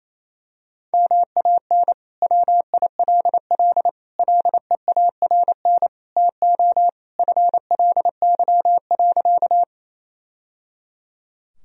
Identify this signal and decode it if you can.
Morse code, heard as a keyed tone, about 28 wpm, 700 Hz. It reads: MAN WILL LEARN TO FLY.
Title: Sentences from Top 1000 Words - Review Part 2 - 28wpm